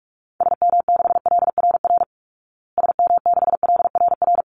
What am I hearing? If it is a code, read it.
Morse code, heard as a keyed tone, about 44 wpm, 700 Hz. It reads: SM6LRR SM6LRR
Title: Call Signs Repeated 3x - 45wpm